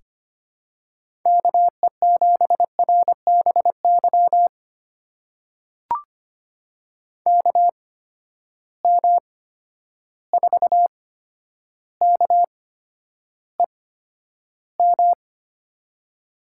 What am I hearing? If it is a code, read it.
Morse code, heard as a keyed tone, about 25 wpm, spaced as for 11 wpm, 700 Hz. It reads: KE7RBY KM4KEM
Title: USA Call Signs - 6 Characters - 8x Character Spacing - 25wpm